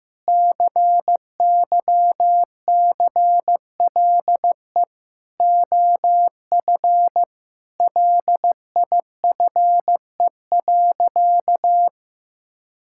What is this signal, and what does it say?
Morse code, heard as a keyed tone, about 16 wpm, 700 Hz. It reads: CYCLE OF LIFE.
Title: Sentences from Top 1000 Words - Review Part 6 - 15wpm